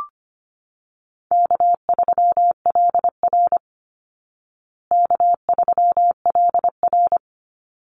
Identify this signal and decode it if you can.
Morse code, heard as a keyed tone, about 25 wpm, 700 Hz. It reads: K3LR K3LR
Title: Call Signs Repeated 3x - 25wpm